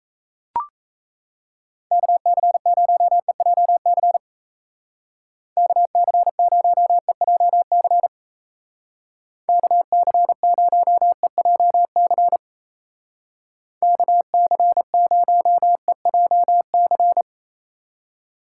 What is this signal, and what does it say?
Morse code, heard as a keyed tone, about 34 wpm, 700 Hz. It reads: KC0EJC KC0EJC KC0EJC KC0EJC
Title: Speed Racing - USA 2x3 Call Signs 28wpm